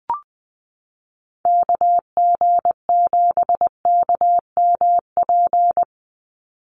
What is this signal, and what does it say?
Morse code, heard as a keyed tone, about 20 wpm, 700 Hz. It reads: KG7KMP
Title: USA Call Signs 20wpm